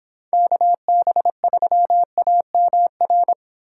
Morse code, heard as a keyed tone, about 29 wpm, 700 Hz. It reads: KB3AMR